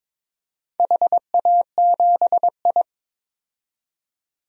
Morse code, heard as a keyed tone, about 21 wpm, 700 Hz. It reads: HA7I
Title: DXCC-100 Call Signs - 4 Characters - 22wpm